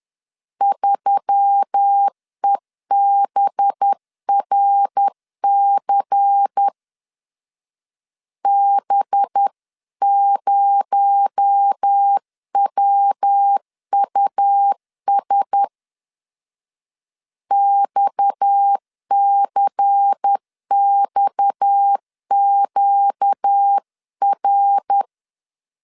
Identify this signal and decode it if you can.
Morse code, heard as a keyed tone, about 10 wpm, 800 Hz. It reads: 3EBRC B0WUS XCXQR
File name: CW / DET / 1 6 1500C S B 10wpm 4137